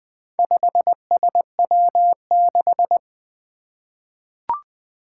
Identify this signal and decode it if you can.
Morse code, heard as a keyed tone, about 21 wpm, 700 Hz. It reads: 5SW6
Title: Four Challenging Letter-Number Combinations 20wpm